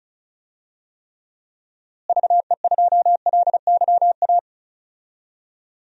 Morse code, heard as a keyed tone, about 36 wpm, 700 Hz. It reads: VE2LYA
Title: DXCC-100 Call Signs - 6 Characters - Repeated 2x - 35wpm